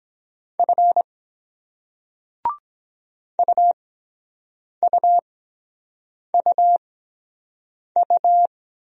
Morse code, heard as a keyed tone, about 23 wpm, spaced as for 7 wpm, 700 Hz. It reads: F UUUU